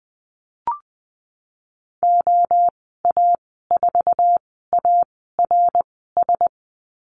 Morse code, heard as a keyed tone, about 19 wpm, spaced as for 14 wpm, 700 Hz. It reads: OA4ARS
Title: 076 Lesson 13g - Letter L - Call Signs 20wpm